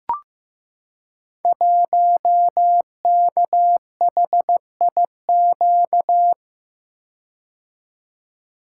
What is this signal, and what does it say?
Morse code, heard as a keyed tone, about 15 wpm, 700 Hz. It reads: 1KHIQ
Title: Code Groups - 5 Letter:Numbers - 15wpm